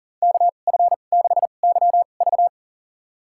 Morse code, heard as a keyed tone, about 40 wpm, spaced as for 29 wpm, 700 Hz. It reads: KF6YV
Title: USA Call Signs - 5 Characters - 2x Character Spacing - 40wpm